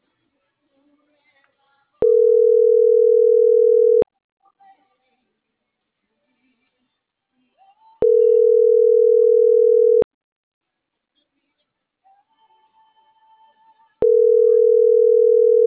Ringback tone (North America), 440 + 480 Hz.